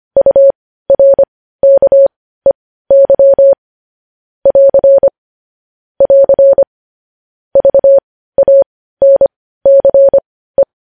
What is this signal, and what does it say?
Morse code, heard as a keyed tone, about 25 wpm, spaced as for 15 wpm, 550 Hz. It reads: URKEY <AR> <AR> VANCE
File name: podcast / News headlines-1x-25x15